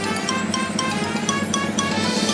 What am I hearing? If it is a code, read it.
There is no code.